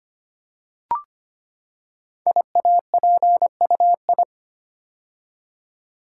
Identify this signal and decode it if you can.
Morse code, heard as a keyed tone, about 25 wpm, 700 Hz. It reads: IAPUI
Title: Code Groups - 5 Letters - 25wpm